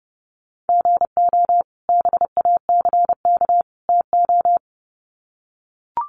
Morse code, heard as a keyed tone, about 30 wpm, 700 Hz. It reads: GO BACK TO
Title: Sets of 3 Words 30wpm